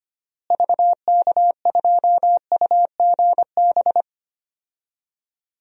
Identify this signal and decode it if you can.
Morse code, heard as a keyed tone, about 25 wpm, 700 Hz. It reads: VK2UGB
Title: DXCC-100 Call Signs - 6 Characters - Repeated 3x - 25wpm